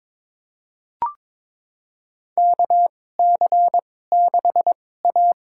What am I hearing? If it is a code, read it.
Morse code, heard as a keyed tone, about 21 wpm, spaced as for 16 wpm, 700 Hz. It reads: KC6A
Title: USA Call Signs - 2x Character Spacing - 22wpm